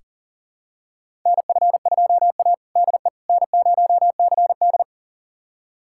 Morse code, heard as a keyed tone, about 40 wpm, 700 Hz. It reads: NF2A DE N0CD
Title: USA Callsign DE Callsign - 4 Characters - 40wpm